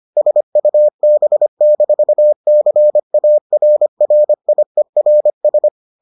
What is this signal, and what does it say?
Morse code, heard as a keyed tone, about 25 wpm, 600 Hz. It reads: SUB-CARRIERS